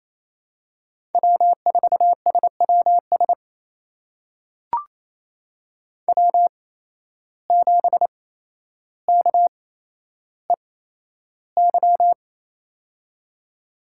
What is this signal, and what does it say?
Morse code, heard as a keyed tone, about 28 wpm, spaced as for 12 wpm, 700 Hz. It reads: W4SWS W7KEY
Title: USA Call Signs - 5 Characters - 8x Character Spacing - 28wpm